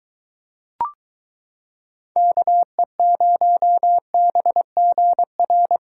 Morse code, heard as a keyed tone, about 23 wpm, 700 Hz. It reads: KE0BGR